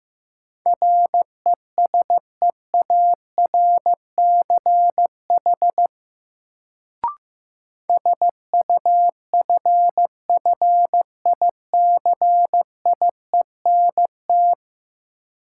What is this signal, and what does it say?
Morse code, heard as a keyed tone, about 15 wpm, 700 Hz. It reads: RESEARCH SUFFICIENT
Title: Sentences from Top 1400 Words - New Word Drill Alt - 15wpm